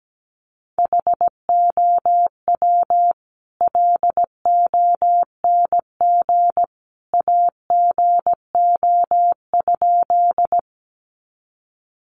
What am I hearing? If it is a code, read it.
Morse code, heard as a keyed tone, about 17 wpm, 700 Hz. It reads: HOW LONG AGO?
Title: Sentences from Top 500 Words 17wpm